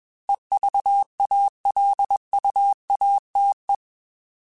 Morse code, heard as a keyed tone, about 21 wpm, 800 Hz. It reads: EVALUATE
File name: Alphabetical / Alphabetical mp3s / Es @20